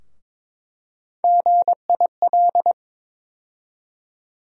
Morse code, heard as a keyed tone, about 22 wpm, 700 Hz. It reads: GIL
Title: USA Call Sign Postfixes - 3 Letters 22wpm